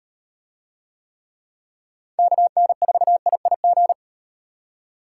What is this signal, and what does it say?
Morse code, heard as a keyed tone, about 36 wpm, 700 Hz. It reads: KN4IIG